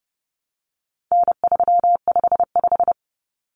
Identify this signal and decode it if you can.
Morse code, heard as a keyed tone, about 30 wpm, 700 Hz. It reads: N355
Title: Common Cut Numbers - 4-Digit 30wpm